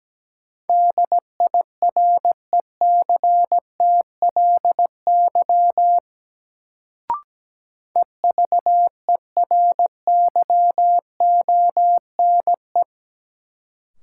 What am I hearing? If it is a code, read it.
Morse code, heard as a keyed tone, about 17 wpm, 700 Hz. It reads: DIRECTLY EVERYONE
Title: Sentences from Top 800 Words - New Word Drill - 17wpm